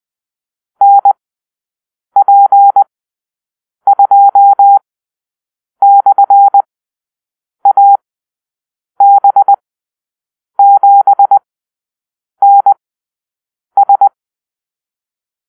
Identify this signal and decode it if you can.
Morse code, heard as a keyed tone, about 20 wpm, spaced as for 7 wpm, 800 Hz. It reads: NP2/AB7NS
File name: arrl / Call-Sign Drill-8